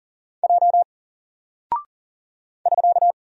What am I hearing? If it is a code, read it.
Morse code, heard as a keyed tone, about 38 wpm, 700 Hz. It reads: J <SK>
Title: Single Character (Letter:Number:Punctuation:Prosign) Alt 40wpm